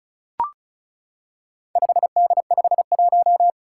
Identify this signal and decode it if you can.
Morse code, heard as a keyed tone, about 36 wpm, 700 Hz. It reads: 5D51
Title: Four Challenging Letter-Number Combinations 35wpm